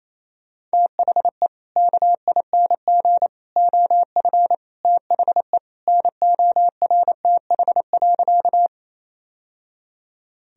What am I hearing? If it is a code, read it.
Morse code, heard as a keyed tone, about 28 wpm, 700 Hz. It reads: THE KING OF THE NORTH.
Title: Sentences from Top 1000 Words - Review Part 1 - 28wpm